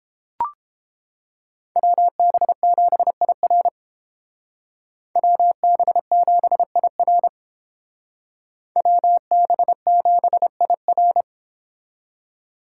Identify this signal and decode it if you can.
Morse code, heard as a keyed tone, about 29 wpm, 700 Hz. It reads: WB7IR WB7IR WB7IR